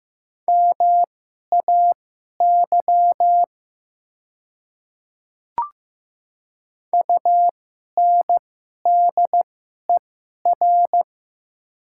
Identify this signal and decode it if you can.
Morse code, heard as a keyed tone, about 15 wpm, spaced as for 11 wpm, 690 Hz. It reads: MAY UNDER
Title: 2x Character Spacing - Top 500 Words - 15wpm